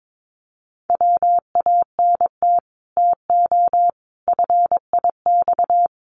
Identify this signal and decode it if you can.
Morse code, heard as a keyed tone, about 22 wpm, 700 Hz. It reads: WANT TO FIX
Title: Sentences from Top 1400 Words 22wpm